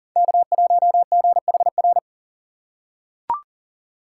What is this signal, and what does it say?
Morse code, heard as a keyed tone, about 40 wpm, 700 Hz. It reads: K1GHR
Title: USA Call Signs - 5 Characters - 4x Character Spacing - 40wpm